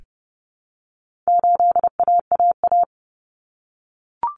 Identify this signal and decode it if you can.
Morse code, heard as a keyed tone, about 30 wpm, 700 Hz. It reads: 8AAA
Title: Common Cut Numbers - 4-Digit 30wpm